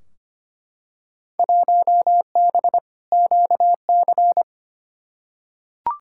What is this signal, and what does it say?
Morse code, heard as a keyed tone, about 25 wpm, 700 Hz. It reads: 1B QC
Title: Field Day Exchanges Encore - 25wpm